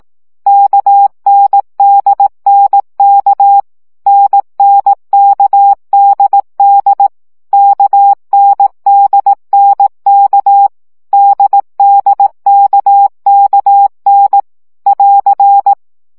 Morse code, heard as a keyed tone, about 18 wpm, 800 Hz. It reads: KNDNK NNKDD KNDNK DDKKN <AR>
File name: CW gyakorlas n d k 15 baud